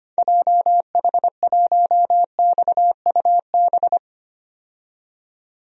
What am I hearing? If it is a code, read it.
Morse code, heard as a keyed tone, about 25 wpm, 700 Hz. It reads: JH1XUB
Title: DXCC-100 Call Signs - 6 Characters - 25wpm